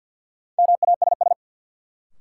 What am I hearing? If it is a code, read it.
Morse code, heard as a keyed tone, about 51 wpm, 700 Hz. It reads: MASS
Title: Sets of 5 Words Encore - New Word Drill - 50wpm